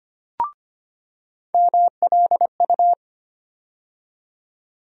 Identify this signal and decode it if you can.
Morse code, heard as a keyed tone, about 25 wpm, 700 Hz. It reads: MLU
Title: USA Call Sign Postfixes - 3 Letters 25wpm